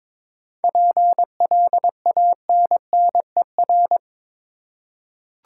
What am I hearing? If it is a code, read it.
Morse code, heard as a keyed tone, about 22 wpm, 700 Hz. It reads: PLANNER
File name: Difficult Words with Hints - 22wpm